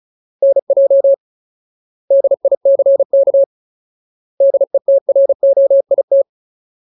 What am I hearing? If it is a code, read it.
Morse code, heard as a keyed tone, about 34 wpm, 550 Hz. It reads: NJ DICK DETROIT
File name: Ultimate Word List / 35WPM Ultimate